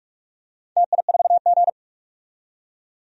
Morse code, heard as a keyed tone, about 44 wpm, 700 Hz. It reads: TI4G
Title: DXCC-100 Call Signs - 4 Characters - 45wpm